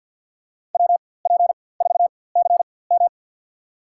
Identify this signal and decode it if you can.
Morse code, heard as a keyed tone, about 53 wpm, spaced as for 24 wpm, 700 Hz. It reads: WP4CM
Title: USA Call Signs - 4x Character Spacing - 50wpm